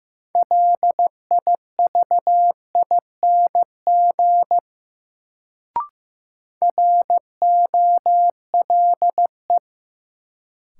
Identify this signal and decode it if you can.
Morse code, heard as a keyed tone, about 15 wpm, 700 Hz. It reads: LIVING ROLE
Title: Sentences from Top 1000 Words - Review Part 6 - New Word Drill - 15wpm